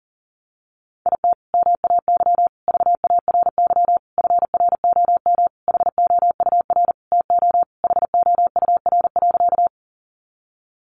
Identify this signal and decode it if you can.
Morse code, heard as a keyed tone, about 37 wpm, 700 Hz. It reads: IT MAY VARY FROM HOUR TO HOUR.
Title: Sentences from Top 1200 Words 40wpm